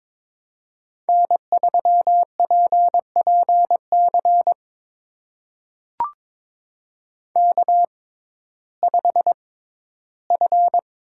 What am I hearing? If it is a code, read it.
Morse code, heard as a keyed tone, about 22 wpm, spaced as for 13 wpm, 700 Hz. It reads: N3PPC K5F
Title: USA Call Signs - 5 Characters - 6x Character Spacing - 22wpm